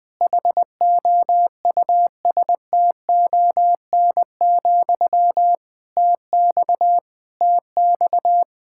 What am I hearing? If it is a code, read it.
Morse code, heard as a keyed tone, about 20 wpm, 700 Hz. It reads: HOUSTON, TX TX
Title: K1USN SST QSO Practice 20wpm